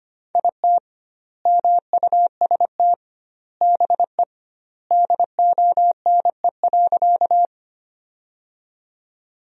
Morse code, heard as a keyed tone, about 25 wpm, 700 Hz. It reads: IT MUST BE DONE.
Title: 2x Word-Spacing - Sentences from Top 500 Words - 25wpm